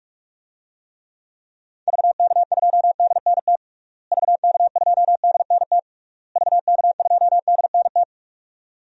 Morse code, heard as a keyed tone, about 41 wpm, 700 Hz. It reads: VK2DNT VK2DNT VK2DNT
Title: DXCC-100 Call Signs - 6 Characters - Repeated 3x - 45wpm